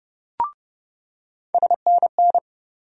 Morse code, heard as a keyed tone, about 30 wpm, 700 Hz. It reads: SNN
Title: USA Call Sign Postfixes - 3 Letters 30wpm